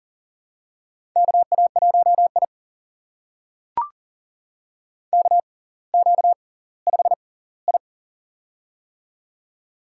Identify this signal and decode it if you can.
Morse code, heard as a keyed tone, about 39 wpm, spaced as for 20 wpm, 690 Hz. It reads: KA1I KQ5I
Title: USA Call Signs - 4 Characters - 6x Character Spacing - 40wpm